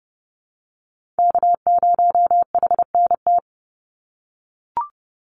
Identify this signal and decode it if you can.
Morse code, heard as a keyed tone, about 30 wpm, 700 Hz. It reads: K0HNT
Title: USA Call Signs 30wpm